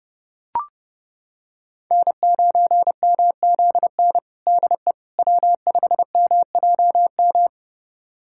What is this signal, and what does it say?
Morse code, heard as a keyed tone, about 30 wpm, 700 Hz. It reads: N9MZN DE W5MJM
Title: USA Callsign DE Callsign - 5 Characters - 30wpm